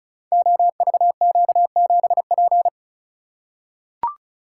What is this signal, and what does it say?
Morse code, heard as a keyed tone, about 36 wpm, 700 Hz. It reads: OVQ7P
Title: Code Groups - 5 Letter:Numbers - 35wpm